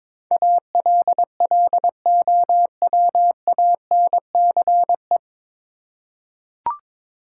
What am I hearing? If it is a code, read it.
Morse code, heard as a keyed tone, about 22 wpm, 700 Hz. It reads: ALLOWANCE
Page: Morse Code Ninja practice 22wpm